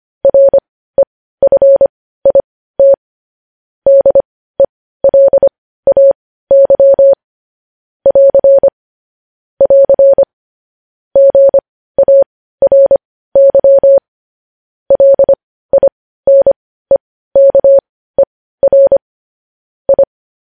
Morse code, heard as a keyed tone, about 25 wpm, spaced as for 15 wpm, 550 Hz. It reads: REFIT DELAY <AR> <AR> GARY LINEKER I
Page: News headlines in morse code at 15 wpm